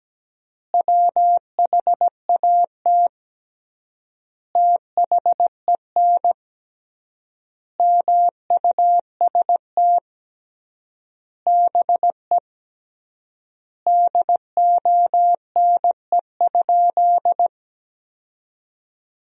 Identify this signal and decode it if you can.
Morse code, heard as a keyed tone, about 17 wpm, 700 Hz. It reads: WHAT THEN MUST BE DONE?
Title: 3x Word-Spacing - Sentences from Top 500 Words - 17wpm